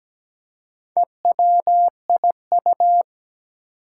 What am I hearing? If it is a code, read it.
Morse code, heard as a keyed tone, about 17 wpm, 700 Hz. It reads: EWIU
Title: Four Challenging Letter-Number Combinations 17wpm